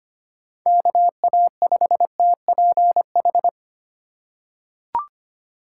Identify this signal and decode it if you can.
Morse code, heard as a keyed tone, about 25 wpm, 700 Hz. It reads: KA5TPH
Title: USA Call Signs - 2x Character Spacing - 25wpm